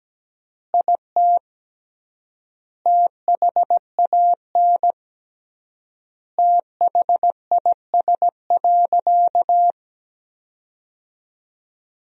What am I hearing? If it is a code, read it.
Morse code, heard as a keyed tone, about 17 wpm, 700 Hz. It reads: IT THAN THIS.
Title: 3x Word-Spacing - Sentences from Top 100 Words - 17wpm